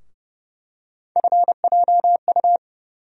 Morse code, heard as a keyed tone, about 29 wpm, 700 Hz. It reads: FJU
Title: USA Call Sign Postfixes - 3 Letters 30wpm